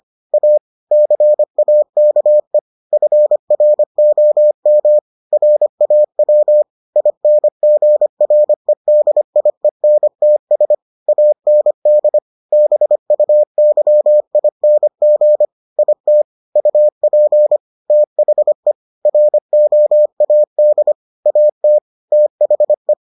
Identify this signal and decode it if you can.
Morse code, heard as a keyed tone, about 25 wpm, 600 Hz. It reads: A CAKE FROM RAW INGREDIENTS AND BUYING IT UP THE ROAD AT THE